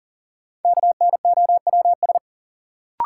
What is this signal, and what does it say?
Morse code, heard as a keyed tone, about 40 wpm, 700 Hz. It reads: KNOWS
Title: Sentences from Top 1000 Words - Review Part 1 - New Word Drill - 40wpm